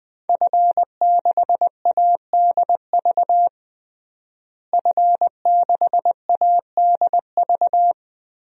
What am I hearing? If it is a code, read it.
Morse code, heard as a keyed tone, about 21 wpm, 700 Hz. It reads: F6ADV F6ADV
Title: DXCC-100 Call Signs - 5 Characters - Repeated 3x - 20wpm